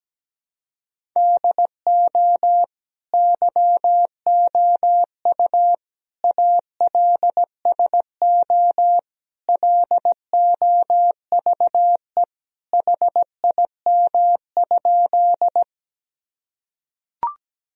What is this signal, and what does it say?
Morse code, heard as a keyed tone, about 17 wpm, 700 Hz. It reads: DO YOU ALSO LOVE HIM?
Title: Sentences from Top 1000 Words - Review Part 1 - 17wpm